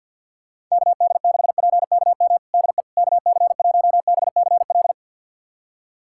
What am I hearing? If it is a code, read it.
Morse code, heard as a keyed tone, about 46 wpm, 700 Hz. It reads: KD6PKM DE KC1BCL
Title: USA Callsign DE Callsign 50wpm